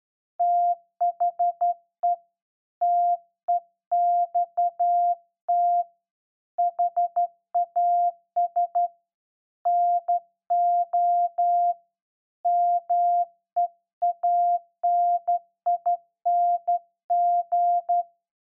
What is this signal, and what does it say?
Morse code, heard as a keyed tone, about 12 wpm, 700 Hz. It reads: THE TEXT HAS NO MEANING
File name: Morse Code01-2(Slow)